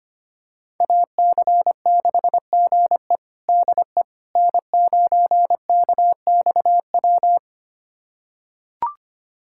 Morse code, heard as a keyed tone, about 25 wpm, 700 Hz. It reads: AC6GE DE N9KXW
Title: USA Callsign DE Callsign - 5 Characters - 25wpm